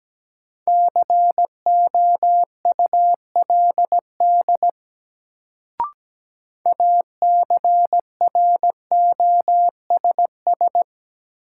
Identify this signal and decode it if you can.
Morse code, heard as a keyed tone, about 17 wpm, 700 Hz. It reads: COULD ACROSS
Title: Sets of 2 Words - New Word Drill - 17wpm